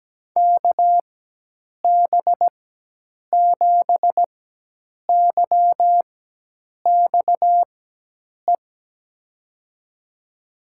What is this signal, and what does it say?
Morse code, heard as a keyed tone, about 17 wpm, spaced as for 8 wpm, 700 Hz. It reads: KB7YXE
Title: USA Call Signs - 6 Characters - 4x Character Spacing - 17wpm